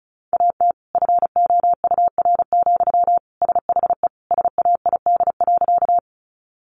Morse code, heard as a keyed tone, about 34 wpm, 700 Hz. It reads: AT FOUR, SHE SAID.
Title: Sentences from Top 300 Words 35wpm